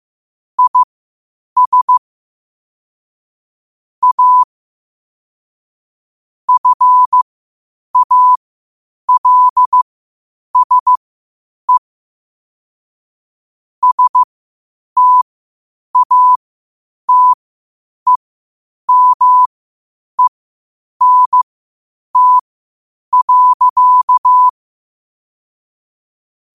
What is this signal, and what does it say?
Morse code, heard as a keyed tone, about 15 wpm, spaced as for 8 wpm, 1000 Hz. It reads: IS A FALSE STATEMENT.